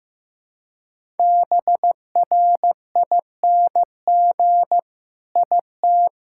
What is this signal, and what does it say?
Morse code, heard as a keyed tone, about 16 wpm, 700 Hz. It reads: BRING IT
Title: Sentences from Top 1000 Words - Review Part 5 - 15wpm